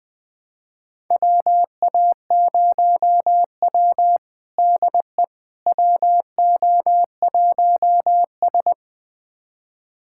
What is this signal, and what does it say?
Morse code, heard as a keyed tone, about 20 wpm, 700 Hz. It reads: WA0W DE WO1S
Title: USA Callsign DE Callsign - 4 Characters - 20wpm